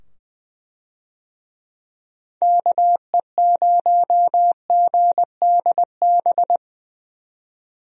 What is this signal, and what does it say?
Morse code, heard as a keyed tone, about 20 wpm, 700 Hz. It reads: KE0GDB
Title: USA Call Signs - 6 Characters - 20wpm